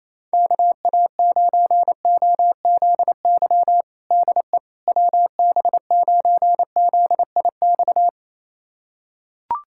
Morse code, heard as a keyed tone, about 28 wpm, 700 Hz. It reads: KA9OZY DE WB9ZIX
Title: USA Callsign DE Callsign - 6 Characters - 28wpm